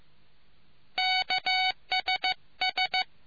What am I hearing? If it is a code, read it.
Morse code, heard as a keyed tone, about 15 wpm, 750 Hz. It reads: KSS